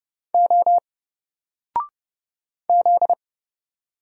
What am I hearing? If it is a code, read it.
Morse code, heard as a keyed tone, about 31 wpm, 700 Hz. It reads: O Z
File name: Single-Letter-Number-30wpm